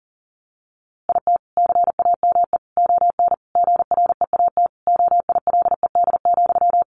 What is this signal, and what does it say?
Morse code, heard as a keyed tone, about 40 wpm, 700 Hz. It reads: IT CAME ON GREAT OILED,
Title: A Sound Of Thunder 40wpm